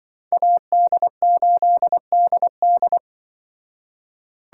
Morse code, heard as a keyed tone, about 24 wpm, 700 Hz. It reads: AD8DD
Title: Speed Racing - USA 2x2 Call Signs 20wpm